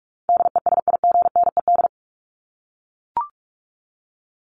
Morse code, heard as a keyed tone, about 44 wpm, 700 Hz. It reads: DESIGNED
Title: ED Suffix - Top 100 Words with Base Words 45wpm